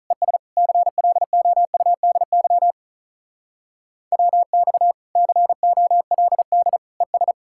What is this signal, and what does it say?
Morse code, heard as a keyed tone, about 44 wpm, 700 Hz. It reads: ES CLOUDY WX COLD ES